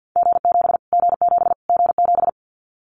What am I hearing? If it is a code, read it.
Morse code, heard as a keyed tone, about 50 wpm, 700 Hz. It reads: G7 G7 G7
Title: Callsign Prefixes for Top 25 DXCC - Repeated 3x - 50wpm